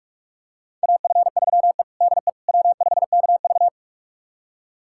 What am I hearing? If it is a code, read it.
Morse code, heard as a keyed tone, about 42 wpm, 700 Hz. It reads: AF3E DE W5KV
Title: USA Callsign DE Callsign - 4 Characters - 45wpm